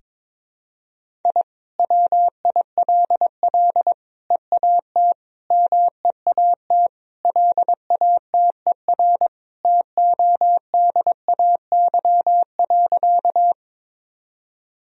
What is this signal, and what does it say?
Morse code, heard as a keyed tone, about 22 wpm, 700 Hz. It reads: I WILL EAT MEAT LATER TODAY.
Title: Sentences from Top 1000 Words - Review Part 3 - 22wpm